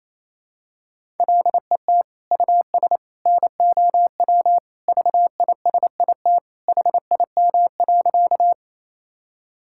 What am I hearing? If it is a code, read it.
Morse code, heard as a keyed tone, about 28 wpm, 700 Hz. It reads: LET US NOW VISIT HIM.
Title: Sentences from Top 1000 Words - Review Part 6 - 28wpm